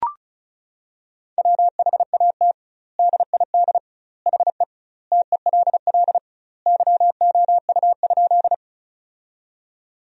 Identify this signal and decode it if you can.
Morse code, heard as a keyed tone, about 36 wpm, 700 Hz. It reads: WHAT DID HE TELL YOU?
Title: 2x Word-Spacing - Sentences from Top 200 Words - 35wpm